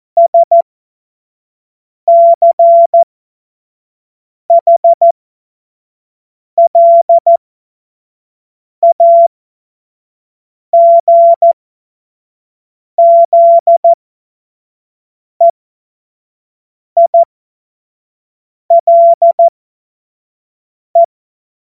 Morse code, heard as a keyed tone, about 14 wpm, spaced as for 5 wpm, 680 Hz. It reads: SCHLAGZEILE